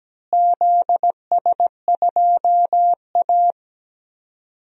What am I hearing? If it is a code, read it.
Morse code, heard as a keyed tone, about 17 wpm, 700 Hz. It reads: ZS2A
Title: DXCC-100 Call Signs - 4 Characters - Repeated 2x - 17wpm